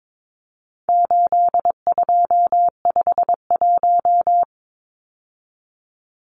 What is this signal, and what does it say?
Morse code, heard as a keyed tone, about 22 wpm, 700 Hz. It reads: 8251
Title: Numbers - 4-Digit 22wpm